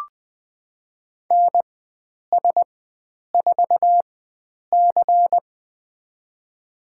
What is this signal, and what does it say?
Morse code, heard as a keyed tone, about 21 wpm, spaced as for 9 wpm, 700 Hz. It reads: NS4C
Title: USA Call Signs - 4 Characters - 4x Character Spacing - 20wpm